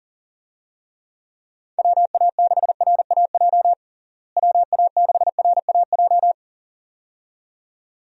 Morse code, heard as a keyed tone, about 40 wpm, 700 Hz. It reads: WA6RAJ WA6RAJ
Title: USA Call Signs - 6 Characters - Repeated 2x - 40wpm